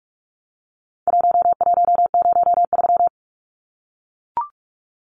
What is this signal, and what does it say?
Morse code, heard as a keyed tone, about 44 wpm, 700 Hz. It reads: 1103
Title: Numbers - Contest Style - 4-Digit 45wpm